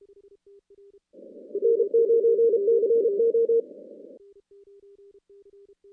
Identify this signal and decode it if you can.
Morse code, heard as a keyed tone, about 34 wpm, 460 Hz. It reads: N9CO
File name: contests / CW-SS-k5tr- -2100